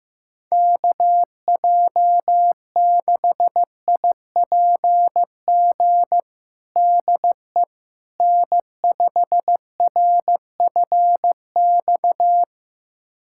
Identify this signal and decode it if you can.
Morse code, heard as a keyed tone, about 15 wpm, 700 Hz. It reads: KJ6IPG DE N5RFX
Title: USA Callsign DE Callsign 15wpm